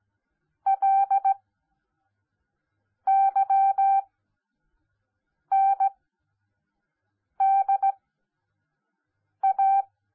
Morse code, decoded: LYNDA